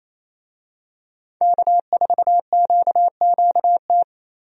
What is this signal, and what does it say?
Morse code, heard as a keyed tone, about 28 wpm, 700 Hz. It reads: K4QQT
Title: USA Call Signs - 5 Characters - 28wpm